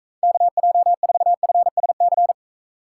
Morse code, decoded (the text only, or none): KJ4FSC